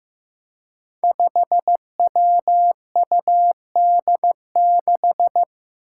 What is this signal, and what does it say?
Morse code, heard as a keyed tone, about 15 wpm, 700 Hz. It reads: 5WUD6
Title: Five Challenging Letter-Number Combinations 15wpm